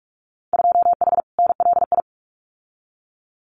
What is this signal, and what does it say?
Morse code, heard as a keyed tone, about 46 wpm, 700 Hz. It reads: 2H NLI
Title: Winter Field Day Exchanges Encore 45wpm